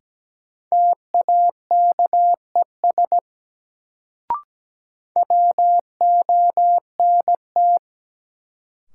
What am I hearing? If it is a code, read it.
Morse code, heard as a keyed tone, about 17 wpm, 700 Hz. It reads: TAKES WONT
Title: Sentences from Top 1000 Words - Review Part 6 - New Word Drill - 17wpm